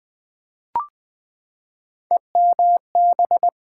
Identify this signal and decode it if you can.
Morse code, heard as a keyed tone, about 20 wpm, 700 Hz. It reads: EMB
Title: Build A Word - 3 Letters At A Time - 6-Letter Words - 20wpm